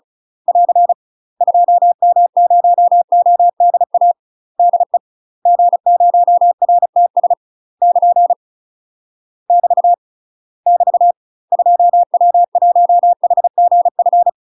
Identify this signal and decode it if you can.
Morse code, heard as a keyed tone, about 34 wpm, 700 Hz. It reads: <AR> 2M0ODA DE G0RTS <KN> <BT> <BT> 2W1HGF
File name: UK MASTER QSOs TRAINING / 35WPM / UK MASTER QSOs 0039